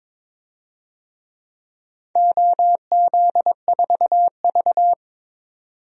Morse code, heard as a keyed tone, about 22 wpm, 700 Hz. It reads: OZ4V
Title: DXCC-100 Call Signs - 4 Characters - 22wpm